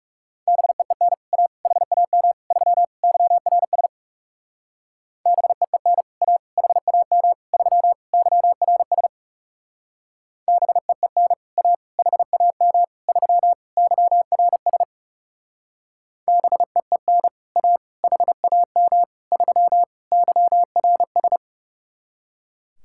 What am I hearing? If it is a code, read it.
Morse code, heard as a keyed tone, about 38 wpm, 700 Hz. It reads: BEEN A HAM 3 YRS BEEN A HAM 3 YRS BEEN A HAM 3 YRS BEEN A HAM 3 YRS